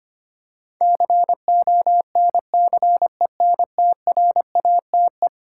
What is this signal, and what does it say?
Morse code, heard as a keyed tone, about 25 wpm, 700 Hz. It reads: CONCENTRATE
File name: practice / 25wpm / Eleven Letter Words - 25wpm